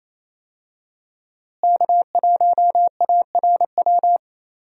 Morse code, heard as a keyed tone, about 28 wpm, 700 Hz. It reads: K1ARW
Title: USA Call Signs - 5 Characters - Repeated 3x - 28wpm